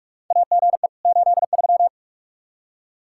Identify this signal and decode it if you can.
Morse code, heard as a keyed tone, about 44 wpm, 700 Hz. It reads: AGE 83
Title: K1USN SST QSO Practice 45wpm